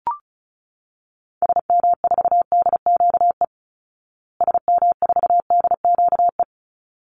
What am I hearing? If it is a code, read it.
Morse code, heard as a keyed tone, about 34 wpm, 700 Hz. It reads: SM4DQE SM4DQE
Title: Call Signs Repeated 3x - 35wpm